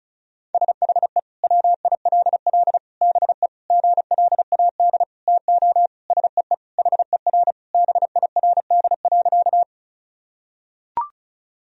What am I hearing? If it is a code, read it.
Morse code, heard as a keyed tone, about 36 wpm, 700 Hz. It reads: SHE WILL BE GLAD TO SEE HER BIRD.
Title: Sentences from Top 1000 Words - Review Part 2 - 35wpm